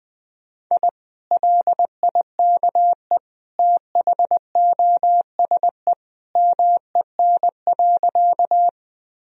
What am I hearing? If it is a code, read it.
Morse code, heard as a keyed tone, about 20 wpm, 700 Hz. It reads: I LIKE THOSE MEN.
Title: Sentences from Top 1000 Words - Review Part 1 - 20wpm